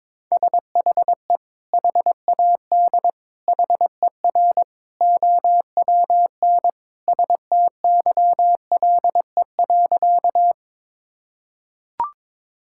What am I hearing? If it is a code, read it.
Morse code, heard as a keyed tone, about 21 wpm, 700 Hz. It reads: SHE HAD HER OWN STYLE.
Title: Sentences from Top 1000 Words - Review Part 5 - 22wpm